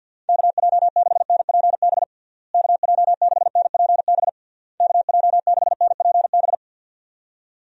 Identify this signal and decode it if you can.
Morse code, heard as a keyed tone, about 52 wpm, 700 Hz. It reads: KJ6NPB KJ6NPB KJ6NPB
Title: USA Call Signs - 6 Characters - Repeated 3x - 50wpm